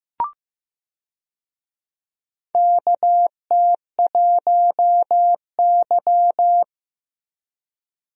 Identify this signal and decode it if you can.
Morse code, heard as a keyed tone, about 15 wpm, 700 Hz. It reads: KT1Y
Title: USA Call Signs - 4 Characters - Repeated 2x - 15wpm